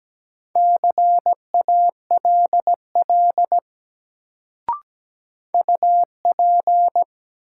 Morse code, heard as a keyed tone, about 17 wpm, 700 Hz. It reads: CALL UP
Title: Sentences from Top 100 Words - New Word Drill - 17wpm